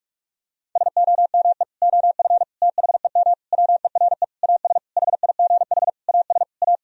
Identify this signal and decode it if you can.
Morse code, heard as a keyed tone, about 48 wpm, 700 Hz. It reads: SOME OF THEM WERE AS HIGH AS A
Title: Sentences from Top 1000 Words - Review Part 1 - 45wpm